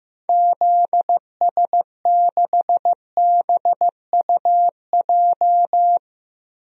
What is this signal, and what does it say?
Morse code, heard as a keyed tone, about 16 wpm, 700 Hz. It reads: ZS6BUJ
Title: DXCC-100 Call Signs - 6 Characters - Repeated 3x - 15wpm